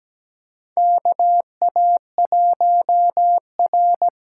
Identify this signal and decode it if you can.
Morse code, heard as a keyed tone, about 17 wpm, 700 Hz. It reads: KA1R
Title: USA Call Signs - 6 Characters - 8x Character Spacing - 17wpm